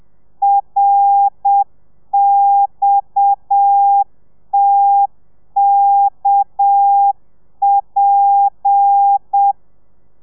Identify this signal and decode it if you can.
Morse code, heard as a keyed tone, about 7 wpm, 800 Hz. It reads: RXTKP
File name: telegrafia / i2tfj / Lez7 30